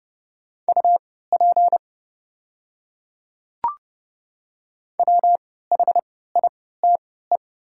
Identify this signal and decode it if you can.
Morse code, heard as a keyed tone, about 30 wpm, spaced as for 17 wpm, 700 Hz. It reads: UP WHITE
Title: 3x Character Spacing - Top 300 Words - 30wpm